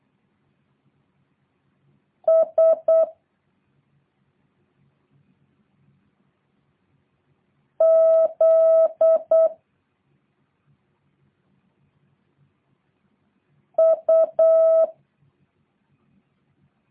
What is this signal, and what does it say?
Morse code, heard as a keyed tone, about 8 wpm, 650 Hz. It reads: S Z U